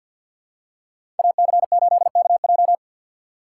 Morse code, heard as a keyed tone, about 53 wpm, 700 Hz. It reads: AC8KJ
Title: USA Call Signs - 5 Characters - 4x Character Spacing - 50wpm